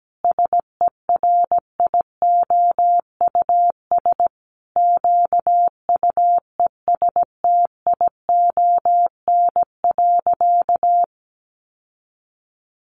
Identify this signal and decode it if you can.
Morse code, heard as a keyed tone, about 17 wpm, 700 Hz. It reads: SERIOUS QUESTION.
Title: Sentences from Top 1400 Words 17wpm